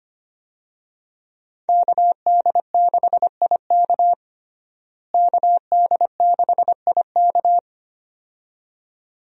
Morse code, heard as a keyed tone, about 25 wpm, 690 Hz. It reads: KD6IK KD6IK
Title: USA Call Signs - 5 Characters - Repeated 2x - 25wpm